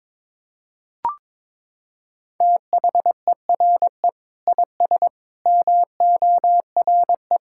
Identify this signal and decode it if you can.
Morse code, heard as a keyed tone, about 21 wpm, 700 Hz. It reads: THERE IS MORE
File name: Sentences with 7-Word Phrases - 22wpm